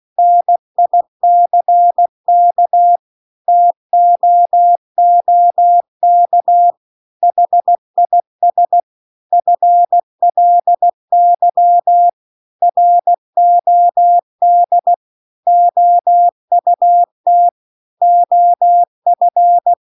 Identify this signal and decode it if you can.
Morse code, heard as a keyed tone, about 16 wpm, 700 Hz. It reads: NICK TOOK HIS FLY ROD OUT OF
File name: CW-The-Big-Twoheaded-River-2-16WPM